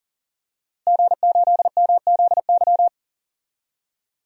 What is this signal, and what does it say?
Morse code, heard as a keyed tone, about 40 wpm, 700 Hz. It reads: G8MZY
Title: DXCC-100 Call Signs - 5 Characters - Repeated 2x - 40wpm